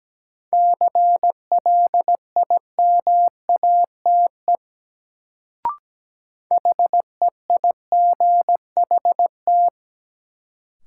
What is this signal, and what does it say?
Morse code, heard as a keyed tone, about 17 wpm, 700 Hz. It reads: CLIMATE HEIGHT
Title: Sentences from Top 1600 Words - New Word Drill - 17wpm